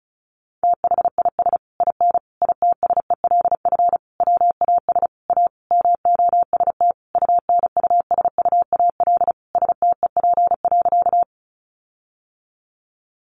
Morse code, heard as a keyed tone, about 34 wpm, 700 Hz. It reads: THIS IN ITSELF WAS A MOST UNUSUAL STEP.
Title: Sentences from Top 1400 Words 35wpm